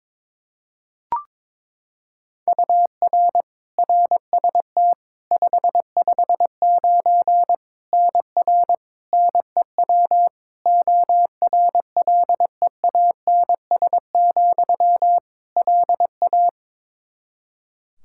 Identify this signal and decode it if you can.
Morse code, heard as a keyed tone, about 22 wpm, 700 Hz. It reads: UR RST 559 NR NEW ORLEANS, LA
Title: K1USN SST QSO Practice 22wpm